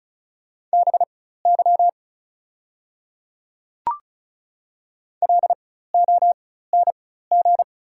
Morse code, heard as a keyed tone, about 34 wpm, spaced as for 16 wpm, 700 Hz. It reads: BY LONG
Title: 4x Character Spacing - Top 200 Words - 35wpm